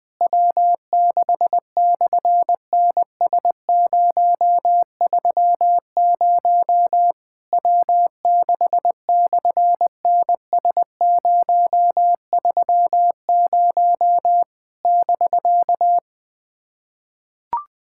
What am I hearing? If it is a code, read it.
Morse code, heard as a keyed tone, about 20 wpm, 700 Hz. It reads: W6/NS030 W6/NS030 <BK>
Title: SOTA QSO Practice 20wpm